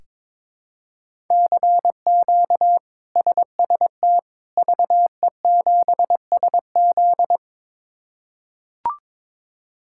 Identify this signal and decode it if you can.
Morse code, heard as a keyed tone, about 22 wpm, 700 Hz. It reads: CQ SST VE7SZ
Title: K1USN SST QSO Practice - 22wpm